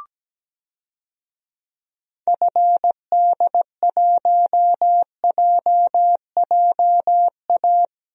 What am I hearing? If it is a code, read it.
Morse code, heard as a keyed tone, about 17 wpm, 700 Hz. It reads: FD1JJA
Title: DXCC-100 Call Signs - 6 Characters - Repeated 3x - 17wpm